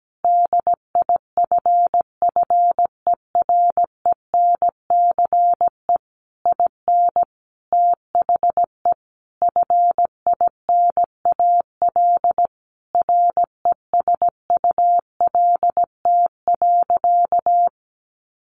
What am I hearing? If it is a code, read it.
Morse code, heard as a keyed tone, about 17 wpm, 700 Hz. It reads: DIFFERENCE IN THE FINAL RESULT.
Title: Sentences from Top 800 Words 17wpm